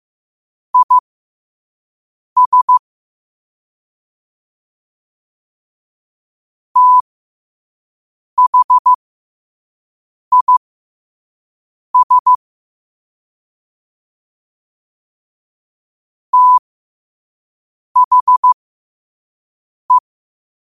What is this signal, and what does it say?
Morse code, heard as a keyed tone, about 15 wpm, spaced as for 5 wpm, 1000 Hz. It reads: IS THIS THE